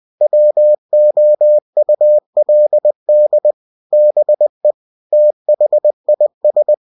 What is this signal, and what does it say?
Morse code, heard as a keyed tone, about 20 wpm, 600 Hz. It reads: WOULD BE THIS
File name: cw-books / winnie-the-pooh / winnie-20wpm-page-0078